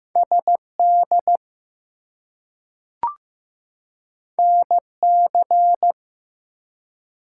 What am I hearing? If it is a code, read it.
Morse code, heard as a keyed tone, about 15 wpm, 700 Hz. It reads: SD NC
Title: US States and Canadian Provinces 15wpm